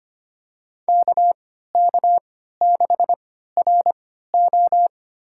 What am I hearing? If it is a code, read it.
Morse code, heard as a keyed tone, about 25 wpm, spaced as for 14 wpm, 700 Hz. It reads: KK6RO
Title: USA Call Signs - 5 Characters - 3x Character Spacing - 25wpm